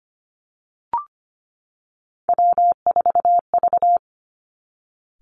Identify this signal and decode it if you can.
Morse code, heard as a keyed tone, about 25 wpm, 700 Hz. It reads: W4V
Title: Three Challenging Letter-Number Combinations 25wpm